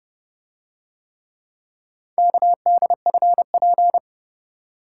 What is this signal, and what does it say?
Morse code, heard as a keyed tone, about 28 wpm, 700 Hz. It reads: KDFP